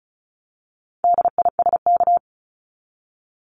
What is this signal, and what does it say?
Morse code, heard as a keyed tone, about 34 wpm, 700 Hz. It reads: DISK
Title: Top 751-1000 Words 35wpm